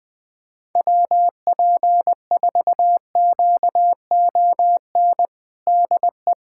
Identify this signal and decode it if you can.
Morse code, heard as a keyed tone, about 21 wpm, 700 Hz. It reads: WP4QON DE
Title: USA Callsign DE Callsign 20wpm